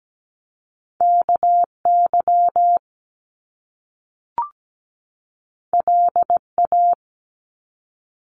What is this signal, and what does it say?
Morse code, heard as a keyed tone, about 17 wpm, 700 Hz. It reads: KY LA
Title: US States 17wpm